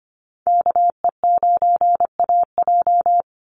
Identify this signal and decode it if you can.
Morse code, heard as a keyed tone, about 25 wpm, 700 Hz. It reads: KE9AJ
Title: NA SOTA Chaser Callsigns 25wpm